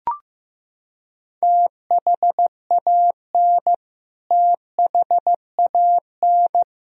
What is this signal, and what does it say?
Morse code, heard as a keyed tone, about 15 wpm, 700 Hz. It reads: THAN THAN
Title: Top 100 Words Repeated 15wpm